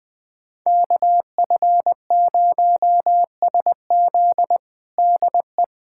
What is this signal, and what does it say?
Morse code, heard as a keyed tone, about 20 wpm, 700 Hz. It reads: KF0SZ DE
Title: USA Callsign DE Callsign - 5 Characters - 20wpm